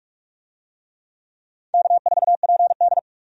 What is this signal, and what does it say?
Morse code, heard as a keyed tone, about 46 wpm, 700 Hz. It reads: K4PD
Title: USA Call Signs - 4 Characters - 45wpm